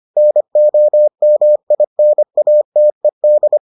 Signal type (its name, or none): Morse code